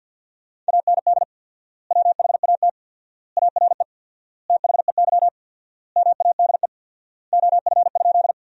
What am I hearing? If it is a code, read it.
Morse code, heard as a keyed tone, about 52 wpm, 700 Hz. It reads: AND WHAT ARE THEY MADE OF?
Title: 4x Word-Spacing - Sentences from Top 200 Words - 50wpm